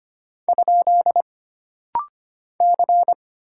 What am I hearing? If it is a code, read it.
Morse code, heard as a keyed tone, about 25 wpm, 700 Hz. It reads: ? C
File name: Single Character (Letter:Number:Punctuation:Prosign) - Rapid-Fire - 25wpm